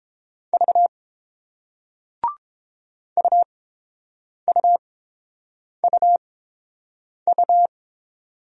Morse code, heard as a keyed tone, about 29 wpm, spaced as for 7 wpm, 690 Hz. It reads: V UUUU